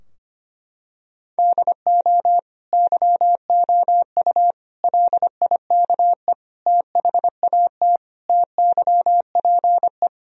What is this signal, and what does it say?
Morse code, heard as a keyed tone, about 25 wpm, 700 Hz. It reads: DO YOU LIKE THAT TYPE
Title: Sentences from Top 1000 Words - Review Part 3 - 25wpm